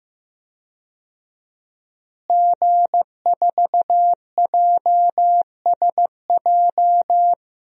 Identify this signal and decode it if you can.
Morse code, heard as a keyed tone, about 15 wpm, 700 Hz. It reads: G4JSJ
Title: DXCC-100 Call Signs - 5 Characters - Repeated 2x - 15wpm